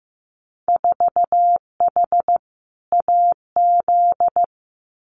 Morse code, heard as a keyed tone, about 15 wpm, 700 Hz. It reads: 4H AZ
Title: Winter Field Day Exchanges Encore 15wpm